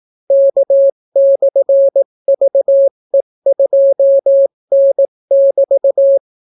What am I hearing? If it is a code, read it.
Morse code, heard as a keyed tone, about 18 wpm, 550 Hz. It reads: K/VE2N<BT>